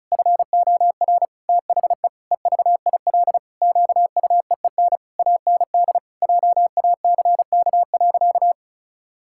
Morse code, heard as a keyed tone, about 36 wpm, 700 Hz. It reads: FOR THE EVIL QUEEN AND JACK.
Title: Pangram Challenge v2 - 35wpm